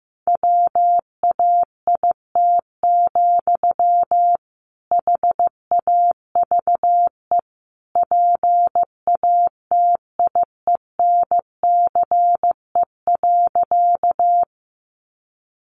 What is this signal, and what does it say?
Morse code, heard as a keyed tone, about 15 wpm, 700 Hz. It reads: WAIT, HAVE PATIENCE.
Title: Sentences from Top 1300 Words 15wpm